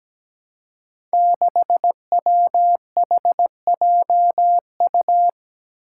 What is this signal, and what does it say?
Morse code, heard as a keyed tone, about 17 wpm, 700 Hz. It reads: 6WHJU
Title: Five Challenging Letter-Number Combinations 17wpm